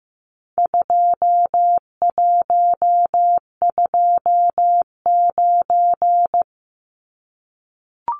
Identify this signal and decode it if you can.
Morse code, heard as a keyed tone, about 15 wpm, 700 Hz. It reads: 2129
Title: Numbers - 4-Digit 15wpm